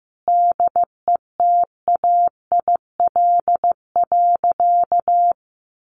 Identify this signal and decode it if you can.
Morse code, heard as a keyed tone, about 15 wpm, 700 Hz. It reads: DETAIL.
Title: Sentences from Top 1500 Words 15wpm